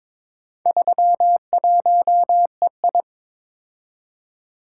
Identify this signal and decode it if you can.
Morse code, heard as a keyed tone, about 22 wpm, 700 Hz. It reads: 31EI
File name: practice / 22wpm / Four Letter-Number Combinations - 22wpm